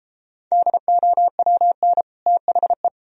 Morse code, heard as a keyed tone, about 31 wpm, 700 Hz. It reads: DOWN THE